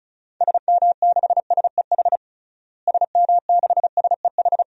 Morse code, heard as a keyed tone, about 34 wpm, 700 Hz. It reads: SM6SEH SM6SEH
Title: DXCC-100 Call Signs - 6 Characters - Repeated 2x - 35wpm